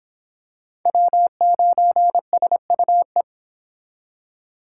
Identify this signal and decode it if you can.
Morse code, heard as a keyed tone, about 25 wpm, 700 Hz. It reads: W9SUE